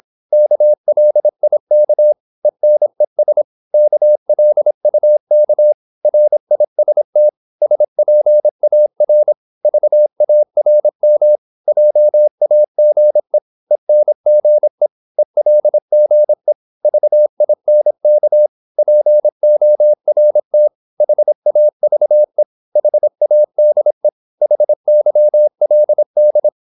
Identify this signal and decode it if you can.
Morse code, heard as a keyed tone, about 26 wpm, 600 Hz. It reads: KLIK ENES KLUK RIST SPAR VARM JAGE ENGE ELGE VINK PORT HAVE HADE HYLD
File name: Korte ord DK 26wpm